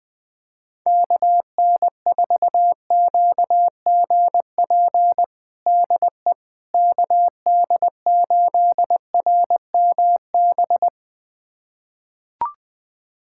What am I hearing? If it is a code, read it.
Morse code, heard as a keyed tone, about 20 wpm, 700 Hz. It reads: KN4QGP DE KD8RMB
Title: USA Callsign DE Callsign 20wpm